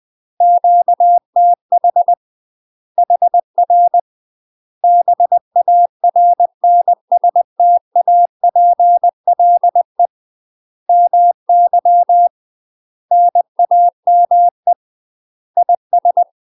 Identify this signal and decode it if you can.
Morse code, heard as a keyed tone, about 20 wpm, 700 Hz. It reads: QTH HR BARNSTAPLE MY NAME IS
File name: UK MASTER QSOs TRAINING/20WPM